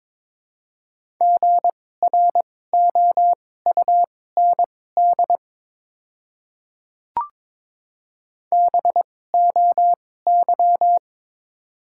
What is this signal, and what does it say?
Morse code, heard as a keyed tone, about 22 wpm, spaced as for 16 wpm, 690 Hz. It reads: GROUND BOY
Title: 2x Character Spacing - Top 500 Words - 22wpm